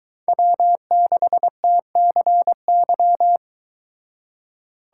Morse code, heard as a keyed tone, about 22 wpm, 700 Hz. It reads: W6TCY